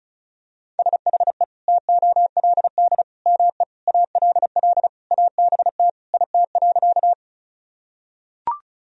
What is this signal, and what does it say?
Morse code, heard as a keyed tone, about 34 wpm, 700 Hz. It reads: SHE TOLD ME ALL ABT IT.
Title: Sentences from Top 400 Words and Abbreviations 35wpm